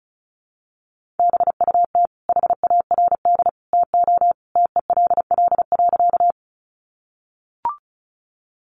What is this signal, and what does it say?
Morse code, heard as a keyed tone, about 34 wpm, 700 Hz. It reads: BUT HARD TO TELL.
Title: Sentences from Top 300 Words 35wpm